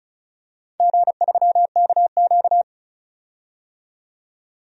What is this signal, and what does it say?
Morse code, heard as a keyed tone, about 36 wpm, 700 Hz. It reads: G3KQ
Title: DXCC-100 Call Signs - 4 Characters - 35wpm